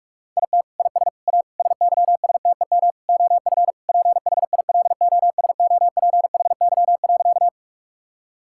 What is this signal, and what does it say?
Morse code, heard as a keyed tone, about 41 wpm, 700 Hz. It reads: IT IS A SYSTEM OF PHILOSOPHY.
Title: Sentences from Top 1000 Words - Review Part 6 - 45wpm